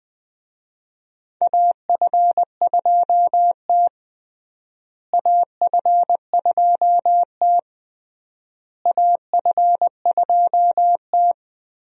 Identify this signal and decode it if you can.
Morse code, heard as a keyed tone, about 20 wpm, 700 Hz. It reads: AF2T AF2T AF2T
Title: USA Call Signs - 4 Characters - Repeated 3x - 20wpm